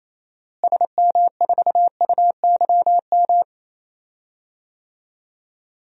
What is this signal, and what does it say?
Morse code, heard as a keyed tone, about 28 wpm, 700 Hz. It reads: SM4UYM
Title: DXCC-100 Call Signs - 6 Characters - Repeated 2x - 28wpm